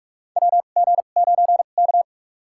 Morse code, heard as a keyed tone, about 46 wpm, spaced as for 33 wpm, 700 Hz. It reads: WG9K
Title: USA Call Signs - 2x Character Spacing - 45wpm